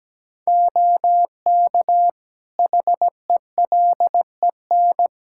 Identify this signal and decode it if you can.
Morse code, heard as a keyed tone, about 17 wpm, 700 Hz. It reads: OK HELEN
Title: Practice QSO - Exchange 3 - More Info - 17wpm